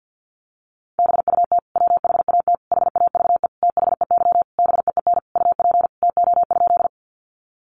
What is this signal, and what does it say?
Morse code, heard as a keyed tone, about 50 wpm, 700 Hz. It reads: BUT WHAT HAVE THEY BEEN UP TO?
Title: Sentences from Top 100 Words 50wpm